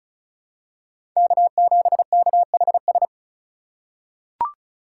Morse code, heard as a keyed tone, about 34 wpm, 700 Hz. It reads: K7KHS
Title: USA Call Signs - 2x Character Spacing - 35wpm